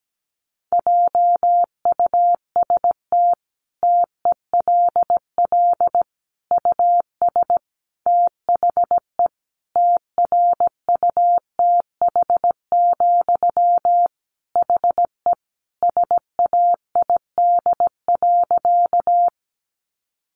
Morse code, heard as a keyed tone, about 17 wpm, 700 Hz. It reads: JUST TELL US THE TRUTH, HE SAID.
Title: Sentences from Top 800 Words 17wpm